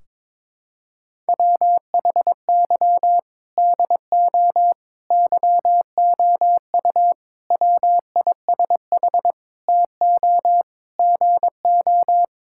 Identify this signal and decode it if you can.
Morse code, heard as a keyed tone, about 21 wpm, 700 Hz. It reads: WHY DO YOU WISH TO GO
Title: Sentences from Top 1600 Words 22wpm